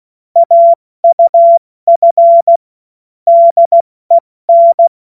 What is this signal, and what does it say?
Morse code, heard as a keyed tone, about 16 wpm, 680 Hz. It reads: AUF DEN
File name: Morsekurs / Witze / Witz 14. März 2023 70 F80